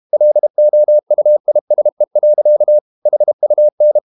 Morse code, heard as a keyed tone, about 32 wpm, 600 Hz. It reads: LOUISE. HUN
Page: Ferie 32 wpm